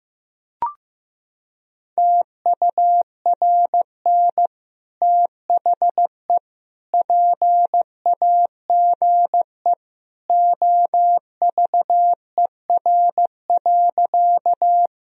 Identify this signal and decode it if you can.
Morse code, heard as a keyed tone, about 16 wpm, 700 Hz. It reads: TURN THE PAGE OVER.
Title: Sentences from Top 1000 Words - Review Part 1 - 15wpm